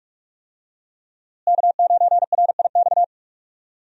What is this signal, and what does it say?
Morse code, heard as a keyed tone, about 46 wpm, 700 Hz. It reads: K9RIX
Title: USA Call Signs - 5 Characters - Repeated 2x - 45wpm